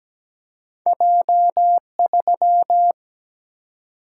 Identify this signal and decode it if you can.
Morse code, heard as a keyed tone, about 17 wpm, 700 Hz. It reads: J3
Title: Single-Letter Single-Number 17wpm